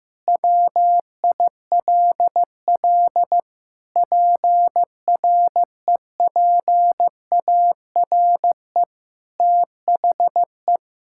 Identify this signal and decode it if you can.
Morse code, heard as a keyed tone, about 15 wpm, 700 Hz. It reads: WILL PREPARE THE
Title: Sentences from Top 1000 Words - Review Part 6 - 15wpm